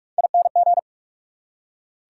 Morse code, heard as a keyed tone, about 48 wpm, 690 Hz. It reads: ING
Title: Prefixes and Suffixes 45wpm